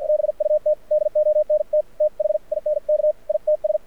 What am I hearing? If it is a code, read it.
Morse code, heard as a keyed tone, about 49 wpm, spaced as for 43 wpm, 600 Hz, with radio noise. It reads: BUT DONT THINK ITS
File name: archived / CRSnet